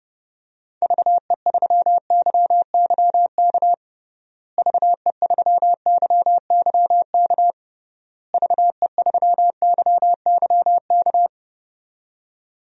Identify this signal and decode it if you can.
Morse code, heard as a keyed tone, about 30 wpm, 700 Hz. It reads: VE3YYK VE3YYK VE3YYK
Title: DXCC-100 Call Signs - 6 Characters - Repeated 3x - 30wpm